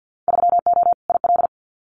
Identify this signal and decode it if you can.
Morse code, heard as a keyed tone, about 50 wpm, 700 Hz. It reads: 3O IL
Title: Winter Field Day Exchanges 50wpm